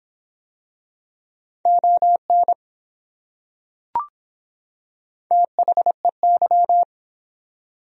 Morse code, heard as a keyed tone, about 24 wpm, 700 Hz. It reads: ON THEY